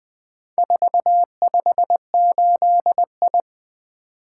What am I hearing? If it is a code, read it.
Morse code, heard as a keyed tone, about 20 wpm, 700 Hz. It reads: 458I
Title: Four Letter-Number Combinations - 20wpm